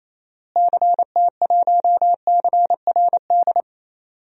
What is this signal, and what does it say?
Morse code, heard as a keyed tone, about 28 wpm, 700 Hz. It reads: CT1CRD
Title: DXCC-100 Call Signs - 6 Characters - Repeated 2x - 28wpm